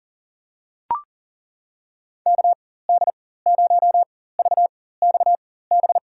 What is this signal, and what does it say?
Morse code, heard as a keyed tone, about 40 wpm, spaced as for 19 wpm, 700 Hz. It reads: KD0VXB
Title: USA Call Signs - 6 Characters - 4x Character Spacing - 40wpm